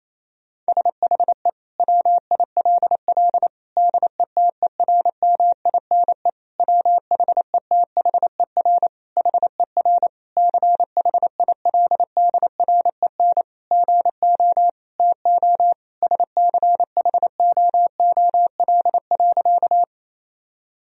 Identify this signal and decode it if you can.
Morse code, heard as a keyed tone, about 28 wpm, 700 Hz. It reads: SHE WILL DETERMINE WHETHER HER CHILDREN GO TO SCHOOL.
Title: Sentences from Top 1000 Words - Review Part 4 - 28wpm